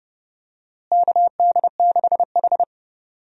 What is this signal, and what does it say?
Morse code, heard as a keyed tone, about 30 wpm, 700 Hz. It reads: KD6H
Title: USA Call Signs - 4 Characters - Repeated 3x - 30wpm